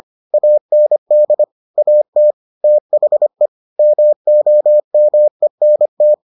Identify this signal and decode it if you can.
Morse code, heard as a keyed tone, about 25 wpm, 600 Hz. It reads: AND AT THE MOMENT